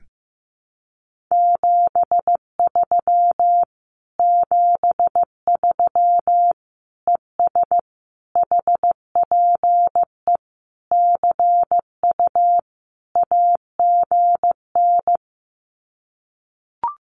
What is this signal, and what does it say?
Morse code, heard as a keyed tone, about 15 wpm, 700 Hz. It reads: 73 73 ES HPE CU AGN
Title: Common QSO Phrases Encore 15wpm